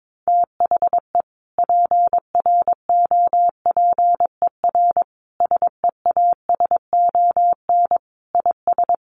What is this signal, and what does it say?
Morse code, heard as a keyed tone, about 22 wpm, 700 Hz. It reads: THE PROPER SEASON IS
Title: Sentences from Top 1400 Words 22wpm